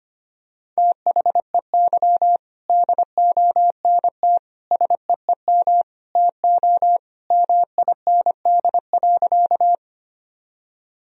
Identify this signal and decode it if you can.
Morse code, heard as a keyed tone, about 25 wpm, 700 Hz. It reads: THEY DONT SEEM TO MIND.
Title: Sentences from Top 1000 Words - Review Part 1 - 25wpm